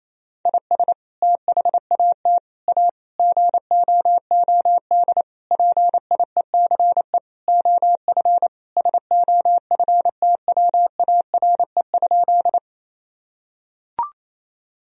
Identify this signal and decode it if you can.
Morse code, heard as a keyed tone, about 28 wpm, 700 Hz. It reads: IS THAT A GOOD PIECE OF SOFTWARE?
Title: Sentences from Top 1000 Words - Review Part 3 - 28wpm